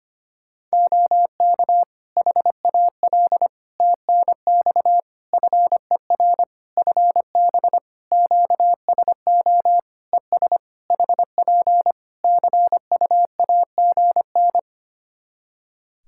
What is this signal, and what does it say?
Morse code, heard as a keyed tone, about 25 wpm, 700 Hz. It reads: OK HAL TNX FER FB QSO ES HP CUAGN
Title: Practice QSO - Exchange 4 - Ending - 25wpm